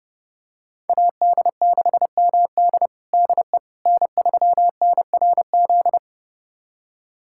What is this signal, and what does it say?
Morse code, heard as a keyed tone, about 30 wpm, 700 Hz. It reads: AD6MD DE N3NRZ
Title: USA Callsign DE Callsign - 5 Characters - 30wpm